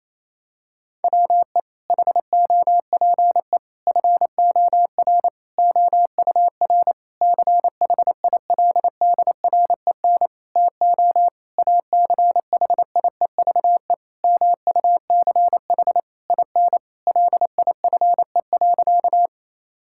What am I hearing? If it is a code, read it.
Morse code, heard as a keyed tone, about 28 wpm, 700 Hz. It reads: WE HOPE FOR OUR CHILDREN TO ACHIEVE MUCH IN LIFE.
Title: Sentences from Top 1000 Words - Review Part 6 - 28wpm